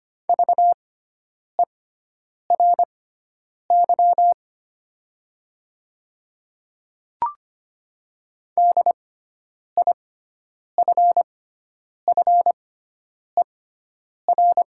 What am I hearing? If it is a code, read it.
Morse code, heard as a keyed tone, about 25 wpm, spaced as for 9 wpm, 700 Hz. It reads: VERY DIFFER
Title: 6x Character Spacing - Top 300 Words - 25wpm